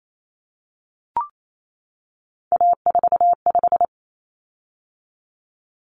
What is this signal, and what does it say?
Morse code, heard as a keyed tone, about 28 wpm, 700 Hz. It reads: A45
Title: Common Cut Numbers - 3-Digit 28wpm